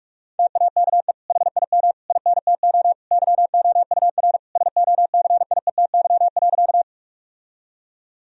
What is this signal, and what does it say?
Morse code, heard as a keyed tone, about 46 wpm, 700 Hz. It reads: TAKE HIM INTO YOUR SOCIETY.
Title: Sentences from Top 1000 Words - Review Part 4 - 45wpm